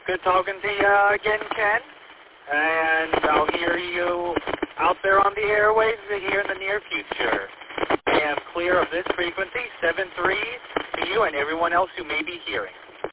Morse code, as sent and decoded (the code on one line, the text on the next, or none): none